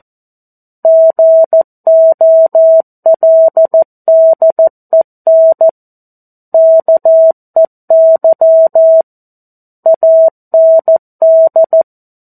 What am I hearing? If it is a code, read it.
Morse code, heard as a keyed tone, about 14 wpm, 650 Hz. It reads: GOLDEN KEY AND
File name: 02-Chapter-02 14 WPM